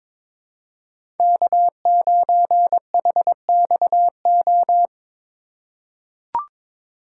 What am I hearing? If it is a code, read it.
Morse code, heard as a keyed tone, about 22 wpm, 700 Hz. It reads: K9HXO
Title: USA Call Signs - 5 Characters - 3x Character Spacing - 22wpm